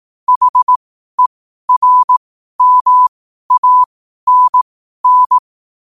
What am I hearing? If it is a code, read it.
Morse code, heard as a keyed tone, about 18 wpm, spaced as for 12 wpm, 1000 Hz. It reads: HERMANN
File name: qotdfiles / 12wpm